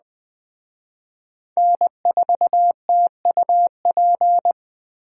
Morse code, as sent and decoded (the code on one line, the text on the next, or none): -. ....- - ..- .--.
N4TUP